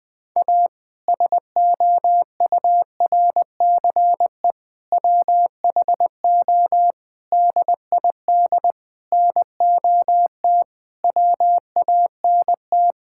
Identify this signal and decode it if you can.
Morse code, heard as a keyed tone, about 20 wpm, 700 Hz. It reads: A SOURCE WHO DID NOT WANT
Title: Sentences with 8-Word Phrases - Spoken Beginning - 20wpm